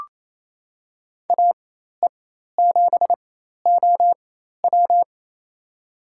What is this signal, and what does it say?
Morse code, heard as a keyed tone, about 27 wpm, spaced as for 13 wpm, 700 Hz. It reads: AE7OW
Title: USA Call Signs - 4x Character Spacing - 28wpm